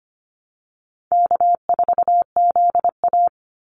Morse code, heard as a keyed tone, about 25 wpm, 700 Hz. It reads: K4ZA
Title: Easy Call Signs Repeated 3x - 25wpm